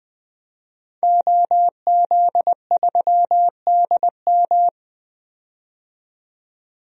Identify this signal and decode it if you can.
Morse code, heard as a keyed tone, about 20 wpm, 700 Hz. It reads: OZ3DM
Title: DXCC-100 Call Signs - 5 Characters - 20wpm